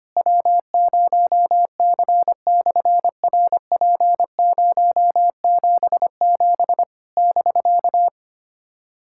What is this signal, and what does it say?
Morse code, heard as a keyed tone, about 26 wpm, 700 Hz. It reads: W0C/RP077 <BK>
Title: SOTA QSO Practice 25wpm